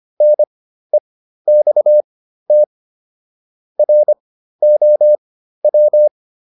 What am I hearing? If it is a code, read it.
Morse code, heard as a keyed tone, about 25 wpm, spaced as for 13 wpm, 600 Hz. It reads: NEXT ROW
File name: SS109 13